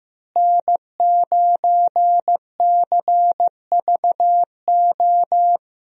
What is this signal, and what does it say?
Morse code, heard as a keyed tone, about 15 wpm, 700 Hz. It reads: N9CVO
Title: USA Call Signs - 5 Characters - Repeated 3x - 15wpm